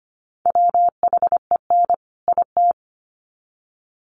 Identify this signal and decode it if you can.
Morse code, heard as a keyed tone, about 25 wpm, 700 Hz. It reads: WHEN IT
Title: Sets of 2 Words 25wpm